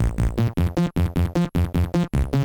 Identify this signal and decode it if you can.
There is no code.